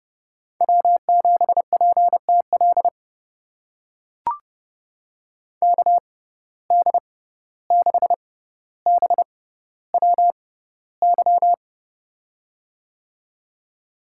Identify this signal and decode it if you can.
Morse code, heard as a keyed tone, about 30 wpm, spaced as for 15 wpm, 700 Hz. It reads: W7PTL KD6BWY
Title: USA Call Signs - 6x Character Spacing - 30wpm